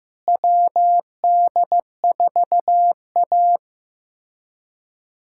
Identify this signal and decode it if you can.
Morse code, heard as a keyed tone, about 15 wpm, 700 Hz. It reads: WD4A
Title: USA Callsign DE Callsign - 4 Characters - 15wpm